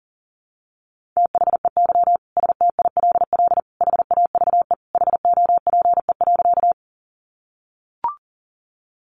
Morse code, heard as a keyed tone, about 40 wpm, 700 Hz. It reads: THEY STILL HAVE HOPE.
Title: Sentences from Top 1000 Words 40wpm